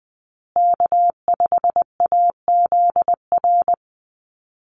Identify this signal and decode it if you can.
Morse code, heard as a keyed tone, about 19 wpm, 700 Hz. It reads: K5AZR
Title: Call Signs Repeated 2x - 20wpm